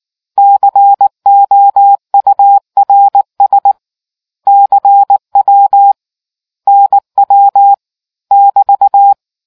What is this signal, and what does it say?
Morse code, heard as a keyed tone, about 19 wpm, 800 Hz. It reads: COURS CW NW <BT>